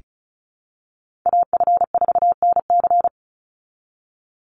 Morse code, heard as a keyed tone, about 34 wpm, 700 Hz. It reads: AF4NC
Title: Call Signs Repeated 3x - 35wpm